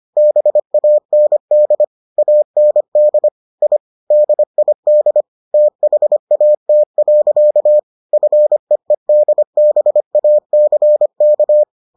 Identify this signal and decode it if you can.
Morse code, heard as a keyed tone, about 25 wpm, 600 Hz. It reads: BAND AND I DID THAT. FEEDBACK